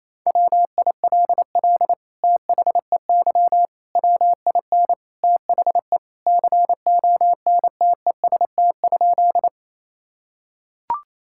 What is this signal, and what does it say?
Morse code, heard as a keyed tone, about 28 wpm, 700 Hz. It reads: WILL THEY WIN THE CONTEST?
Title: Sentences from Top 1600 Words 28wpm